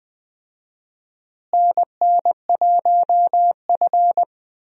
Morse code, heard as a keyed tone, about 20 wpm, 700 Hz. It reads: NN1F
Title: USA Call Signs - 4 Characters - Repeated 2x - 20wpm